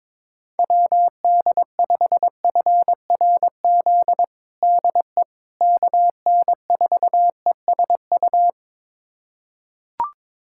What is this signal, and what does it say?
Morse code, heard as a keyed tone, about 22 wpm, 700 Hz. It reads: WD5FRZ DE KN4ESU
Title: USA Callsign DE Callsign 22wpm